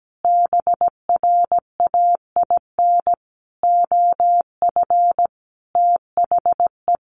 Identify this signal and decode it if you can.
Morse code, heard as a keyed tone, about 17 wpm, 700 Hz. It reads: BRAIN OF THE
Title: Sentences from Top 1100 Words 17wpm